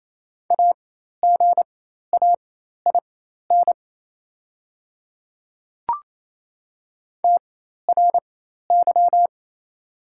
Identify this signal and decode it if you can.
Morse code, heard as a keyed tone, about 28 wpm, spaced as for 13 wpm, 690 Hz. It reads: AGAIN TRY
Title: 4x Character Spacing - Top 300 Words - 28wpm